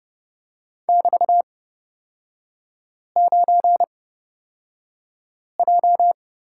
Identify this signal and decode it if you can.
Morse code, heard as a keyed tone, about 29 wpm, spaced as for 5 wpm, 700 Hz. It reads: <BT>9J
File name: Lesson 36e - BT - Three Characters 30wpm